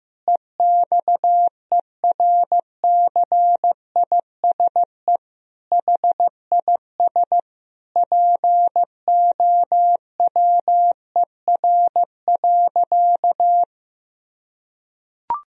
Morse code, heard as a keyed tone, about 14 wpm, 700 Hz. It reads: EXERCISE HIS POWER.
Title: Sentences from Top 1000 Words - Review Part 4 - 15wpm